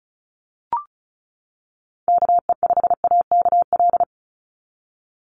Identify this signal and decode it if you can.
Morse code, heard as a keyed tone, about 34 wpm, 700 Hz. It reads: KE5AKL
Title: NA SOTA Chaser Callsigns 35wpm